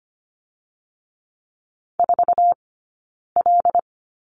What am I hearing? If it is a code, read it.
Morse code, heard as a keyed tone, about 25 wpm, spaced as for 9 wpm, 700 Hz. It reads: 4L
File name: Lesson 15d - Number 5 - Two Characters 25wpm